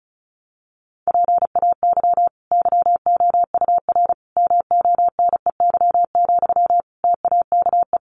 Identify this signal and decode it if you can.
Morse code, heard as a keyed tone, about 34 wpm, 700 Hz. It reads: PAY YOUR MONEY, TAKE
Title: Sentences from Top 900 Words 35wpm